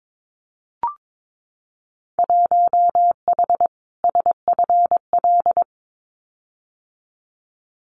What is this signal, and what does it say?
Morse code, heard as a keyed tone, about 22 wpm, 700 Hz. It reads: 1H SFL
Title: Winter Field Day Exchanges 22wpm